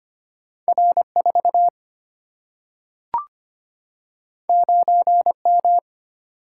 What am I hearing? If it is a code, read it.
Morse code, heard as a keyed tone, about 25 wpm, 700 Hz. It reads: R4 9M
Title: Single-Letter Single-Number 25wpm